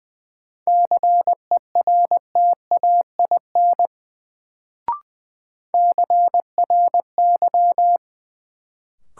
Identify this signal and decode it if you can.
Morse code, heard as a keyed tone, about 20 wpm, 700 Hz. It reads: CERTAIN CRY
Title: Sentences from Top 500 Words - New Word Drill - 20wpm